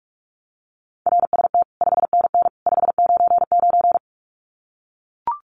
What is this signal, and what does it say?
Morse code, heard as a keyed tone, about 44 wpm, 700 Hz. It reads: RST 5NN 599
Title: Common QSO Phrases 45wpm